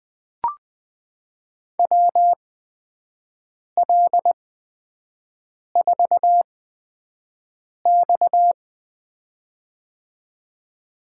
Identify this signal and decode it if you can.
Morse code, heard as a keyed tone, about 20 wpm, spaced as for 5 wpm, 700 Hz. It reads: WL4X
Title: USA Call Signs - 8x Character Spacing - 20wpm